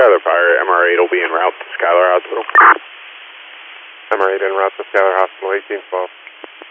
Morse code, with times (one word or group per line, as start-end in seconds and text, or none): none